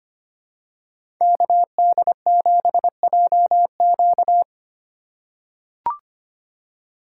Morse code, heard as a keyed tone, about 25 wpm, 700 Hz. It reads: KD7JQ
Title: USA Call Signs - 5 Characters - 4x Character Spacing - 25wpm